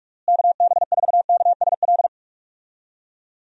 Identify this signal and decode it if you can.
Morse code, heard as a keyed tone, about 46 wpm, 700 Hz. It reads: KB4KSL
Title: USA Call Signs - 6 Characters - Repeated 3x - 45wpm